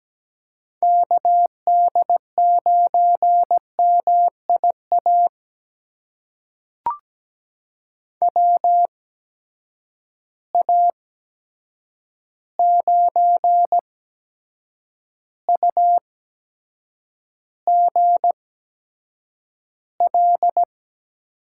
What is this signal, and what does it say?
Morse code, heard as a keyed tone, about 17 wpm, spaced as for 7 wpm, 700 Hz. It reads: KD9MIA WA9UGL
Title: USA Call Signs - 6 Characters - 8x Character Spacing - 17wpm